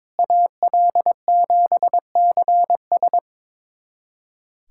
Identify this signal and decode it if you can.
Morse code, heard as a keyed tone, about 22 wpm, 700 Hz. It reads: AL7CS
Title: USA Callsign DE Callsign - 5 Characters - 22wpm